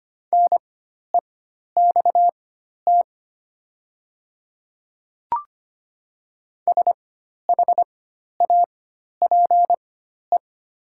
Morse code, heard as a keyed tone, about 25 wpm, spaced as for 12 wpm, 700 Hz. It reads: NEXT SHAPE
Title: 4x Character Spacing - Top 500 Words - 25wpm